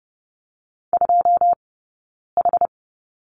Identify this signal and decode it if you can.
Morse code, heard as a keyed tone, about 30 wpm, spaced as for 9 wpm, 700 Hz. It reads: 2H
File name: Lesson 20d - Number 3 - Two Characters 30wpm